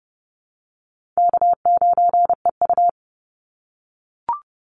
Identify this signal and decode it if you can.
Morse code, heard as a keyed tone, about 30 wpm, 700 Hz. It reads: K9EU
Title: Call Signs - 30wpm